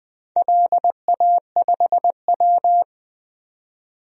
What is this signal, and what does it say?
Morse code, heard as a keyed tone, about 20 wpm, 700 Hz. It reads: LA5W
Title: DXCC-100 Call Signs - 4 Characters - 20wpm